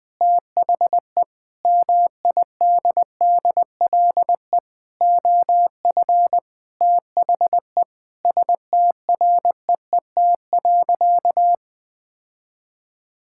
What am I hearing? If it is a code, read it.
Morse code, heard as a keyed tone, about 20 wpm, 700 Hz. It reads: THE MIDDLE OF THE STREET.
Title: Sentences with 4-Word Phrases - 20wpm